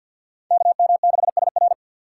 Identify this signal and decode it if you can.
Morse code, heard as a keyed tone, about 52 wpm, 700 Hz. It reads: KM6SR
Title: USA Call Signs - 5 Characters - 3x Character Spacing - 50wpm